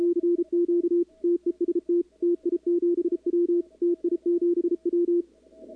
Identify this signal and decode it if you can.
Morse code, heard as a keyed tone, about 34 wpm, 340 Hz, with radio noise. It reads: CQ TEST TI7W TI7W
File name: contests / 2019-cw-wpx-kt5j-k5pi- -2330 0-11